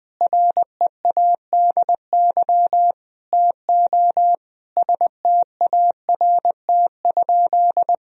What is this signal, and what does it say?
Morse code, heard as a keyed tone, about 21 wpm, 700 Hz. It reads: READY TO START?
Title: Sentences from Top 1000 Words - Review Part 1 - 20wpm